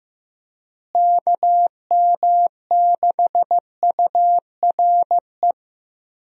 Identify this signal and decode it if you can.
Morse code, heard as a keyed tone, about 15 wpm, 700 Hz. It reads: KM6URE
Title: USA Call Signs - 6 Characters - Repeated 3x - 15wpm